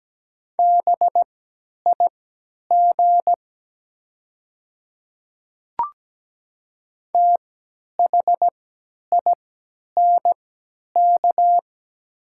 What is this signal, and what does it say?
Morse code, heard as a keyed tone, about 17 wpm, spaced as for 10 wpm, 700 Hz. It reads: BIG THINK
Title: 3x Character Spacing - Top 200 Words - 17wpm